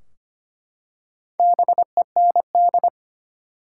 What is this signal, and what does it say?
Morse code, heard as a keyed tone, about 25 wpm, 700 Hz. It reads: BEND
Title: Four Letter Words - 25wpm